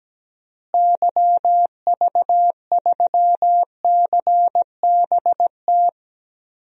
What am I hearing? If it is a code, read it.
Morse code, heard as a keyed tone, about 17 wpm, 700 Hz. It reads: YV3CBT
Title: DXCC-100 Call Signs - 6 Characters - Repeated 3x - 17wpm